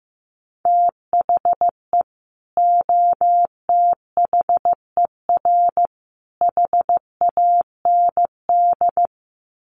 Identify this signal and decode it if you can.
Morse code, heard as a keyed tone, about 15 wpm, 700 Hz. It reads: THE OTHER HAND
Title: Sets of 3 Words 15wpm